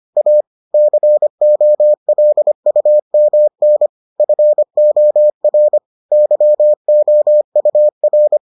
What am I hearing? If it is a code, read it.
Morse code, heard as a keyed tone, about 25 wpm, 600 Hz. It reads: A COLUMN FOR YOUR